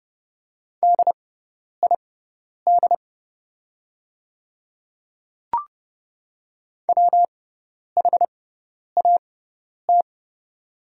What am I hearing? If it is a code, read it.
Morse code, heard as a keyed tone, about 30 wpm, spaced as for 10 wpm, 700 Hz. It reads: DID WHAT
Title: 6x Character Spacing - Top 100 Words - 30wpm